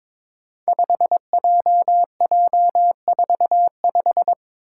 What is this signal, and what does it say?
Morse code, heard as a keyed tone, about 22 wpm, 700 Hz. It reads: 5JJ45
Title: Five Challenging Letter-Number Combinations 22wpm